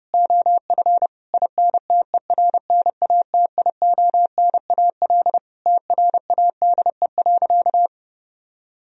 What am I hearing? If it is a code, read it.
Morse code, heard as a keyed tone, about 30 wpm, 700 Hz. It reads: OF INTERNATIONAL TRADE.
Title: Sentences from Top 1000 Words - Review Part 4 - 30wpm